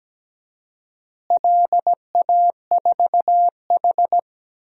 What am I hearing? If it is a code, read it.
Morse code, heard as a keyed tone, about 17 wpm, 700 Hz. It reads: LA4H
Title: DXCC-100 Call Signs - 4 Characters - Repeated 2x - 17wpm